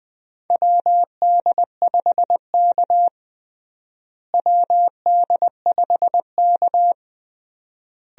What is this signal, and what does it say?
Morse code, heard as a keyed tone, about 21 wpm, 700 Hz. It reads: WD5K WD5K
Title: USA Call Signs - 4 Characters - Repeated 3x - 20wpm